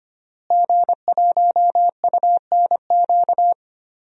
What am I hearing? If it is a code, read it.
Morse code, heard as a keyed tone, about 25 wpm, 700 Hz. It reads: G1UNQ
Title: DXCC-100 Call Signs - 5 Characters - Repeated 3x - 25wpm